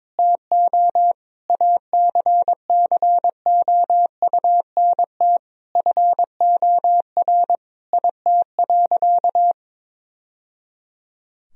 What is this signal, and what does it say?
Morse code, heard as a keyed tone, about 21 wpm, 700 Hz. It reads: TO ACCOUNT FOR IT.
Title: Sentences from Top 1000 Words - Review Part 2 - 22wpm